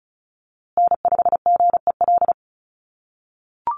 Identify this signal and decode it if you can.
Morse code, heard as a keyed tone, about 34 wpm, 700 Hz. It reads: N5GEL
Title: USA Call Signs Repeated 2x 35wpm